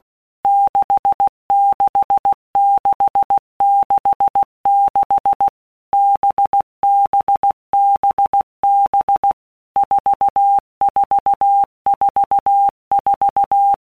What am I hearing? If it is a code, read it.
Morse code, heard as a keyed tone, about 16 wpm, 800 Hz. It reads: 66666 BBBB 4444
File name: Basic Letter Groups By WPM's / Groups@15 / VB46@15